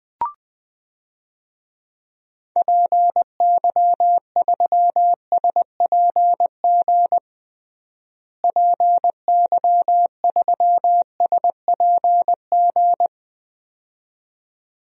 Morse code, heard as a keyed tone, about 20 wpm, 700 Hz. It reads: PY3SPG PY3SPG
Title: DXCC-100 Call Signs - 6 Characters - Repeated 2x - 20wpm